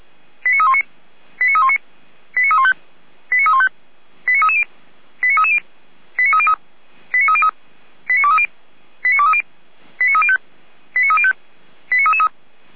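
Five-tone selective call (ZVEI), heard as quick runs of five tones, tones of about 70 ms. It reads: 89319 89319 89315 89315 89330 89330 89303 89303 89230 89230 89305 89305 89333
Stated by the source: ZVEI five-tone call received by SDR.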